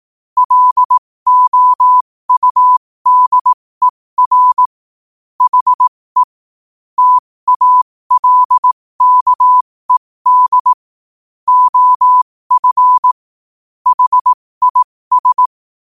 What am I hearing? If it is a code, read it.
Morse code, heard as a keyed tone, about 18 wpm, spaced as for 15 wpm, 1000 Hz. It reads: LOUDER HE TALKED OF HIS
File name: qotdfiles / 15wpm